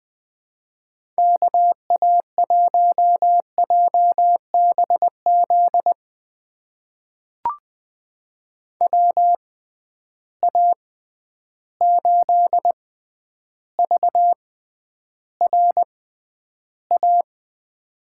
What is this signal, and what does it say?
Morse code, heard as a keyed tone, about 20 wpm, spaced as for 10 wpm, 700 Hz. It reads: KA1JBZ WA8VRA
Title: USA Call Signs - 6 Characters - 6x Character Spacing - 20wpm